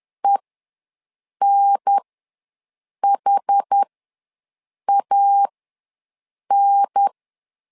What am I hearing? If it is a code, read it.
Morse code, heard as a keyed tone, about 10 wpm, spaced as for 6 wpm, 800 Hz. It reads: ENHAN
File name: LEZIONI / 4 Q ws 10wpm 2306